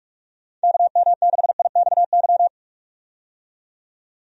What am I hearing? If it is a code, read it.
Morse code, heard as a keyed tone, about 46 wpm, 700 Hz. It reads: KM6IXY
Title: USA Call Signs - 6 Characters - 45wpm